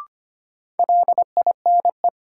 Morse code, heard as a keyed tone, about 25 wpm, 700 Hz. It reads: LINE
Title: Sentences from Top 1200 Words - New Word Drill Alt - 25wpm